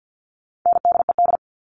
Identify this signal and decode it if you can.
Morse code, heard as a keyed tone, about 50 wpm, 700 Hz. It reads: NDED
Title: All Cut Numbers - 4-Digit 50wpm